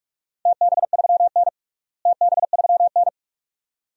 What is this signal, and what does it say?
Morse code, heard as a keyed tone, about 46 wpm, 700 Hz. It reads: TB3N TB3N
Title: DXCC-100 Call Signs - 4 Characters - Repeated 2x - 45wpm